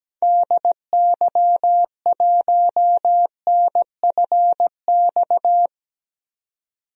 Morse code, heard as a keyed tone, about 17 wpm, 700 Hz. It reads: DY1NFX
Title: DXCC-100 Call Signs - 6 Characters - 17wpm